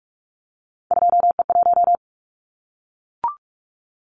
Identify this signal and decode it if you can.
Morse code, heard as a keyed tone, about 44 wpm, 700 Hz. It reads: 2E1
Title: Callsign Prefixes for Top 25 DXCC - Repeated 3x - 45wpm